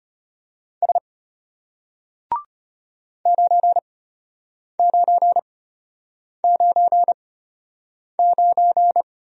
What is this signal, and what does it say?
Morse code, heard as a keyed tone, about 32 wpm, spaced as for 8 wpm, 690 Hz. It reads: S 9999